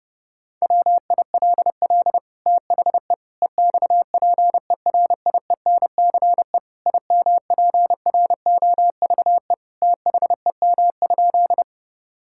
Morse code, heard as a keyed tone, about 30 wpm, 700 Hz. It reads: WILL THE EXPERIENCE IMPROVE THEM?
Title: Sentences from Top 1000 Words - Review Part 5 - 30wpm